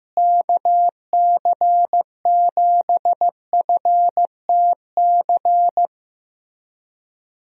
Morse code, heard as a keyed tone, about 16 wpm, 700 Hz. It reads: KC7FTC
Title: USA Call Signs - 6 Characters - Repeated 3x - 15wpm